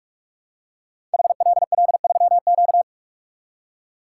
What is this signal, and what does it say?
Morse code, heard as a keyed tone, about 44 wpm, 700 Hz. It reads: HLL3Q
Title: Code Groups - 5 Letter:Numbers - 45wpm